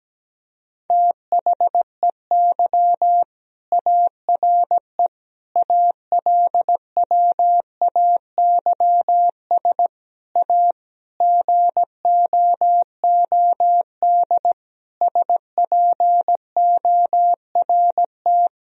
Morse code, heard as a keyed tone, about 17 wpm, 700 Hz. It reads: THEY ARE ALWAYS A GOOD SPORT
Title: Sentences from Top 1000 Words - Review Part 4 - 17wpm